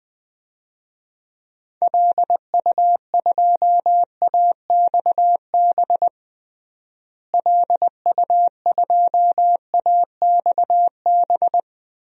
Morse code, heard as a keyed tone, about 20 wpm, 700 Hz. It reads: LU2AXB LU2AXB
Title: DXCC-100 Call Signs - 6 Characters - Repeated 3x - 20wpm